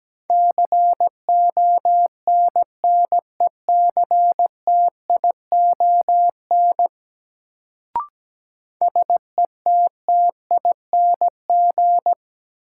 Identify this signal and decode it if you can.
Morse code, heard as a keyed tone, about 17 wpm, 700 Hz. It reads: CONNECTION SETTING
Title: Sentences from Top 1300 Words - New Word Drill - 17wpm